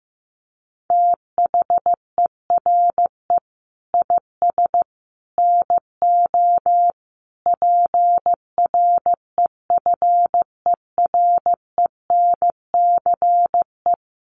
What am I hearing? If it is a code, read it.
Morse code, heard as a keyed tone, about 15 wpm, 700 Hz. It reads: THERE IS NO PREFERENCE
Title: Sentences from Top 1500 Words 15wpm